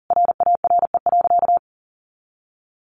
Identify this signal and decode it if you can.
Morse code, heard as a keyed tone, about 40 wpm, 700 Hz. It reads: RARE.
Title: Sentences from Top 1500 Words 40wpm